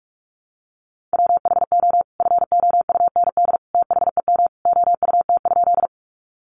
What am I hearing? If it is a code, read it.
Morse code, heard as a keyed tone, about 44 wpm, 700 Hz. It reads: WHO FOUND THEM OUT?
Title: Sentences from Top 300 Words 45wpm